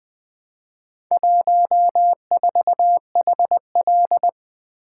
Morse code, heard as a keyed tone, about 20 wpm, 700 Hz. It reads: 14HL
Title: Four Letter-Number Combinations - 20wpm